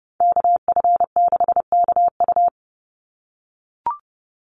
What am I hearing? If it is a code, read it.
Morse code, heard as a keyed tone, about 30 wpm, 700 Hz. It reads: KF6KU
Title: USA Call Signs 30wpm